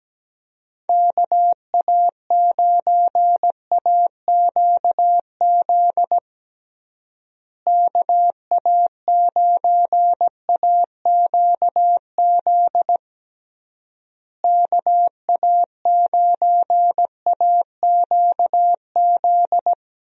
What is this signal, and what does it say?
Morse code, heard as a keyed tone, about 17 wpm, 700 Hz. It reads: KA9AQZ KA9AQZ KA9AQZ
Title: USA Call Signs - 6 Characters - Repeated 3x - 17wpm